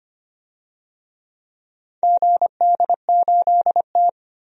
Morse code, heard as a keyed tone, about 25 wpm, 700 Hz. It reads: GD8T
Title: DXCC-100 Call Signs - 4 Characters - 25wpm